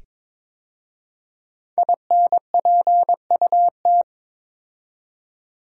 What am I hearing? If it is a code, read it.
Morse code, heard as a keyed tone, about 22 wpm, 700 Hz. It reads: INPUT
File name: 5 Letter Words - First 3 Letters Given - 22wpm